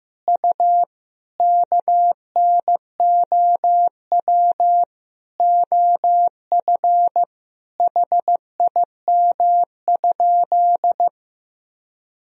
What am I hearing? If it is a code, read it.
Morse code, heard as a keyed tone, about 16 wpm, 700 Hz. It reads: U KNOW OF HIM?
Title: Sentences from Top 100 Words and Abbreviations 15wpm